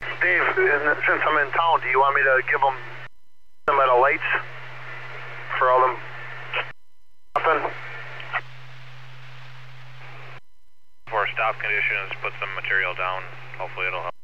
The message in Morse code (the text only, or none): none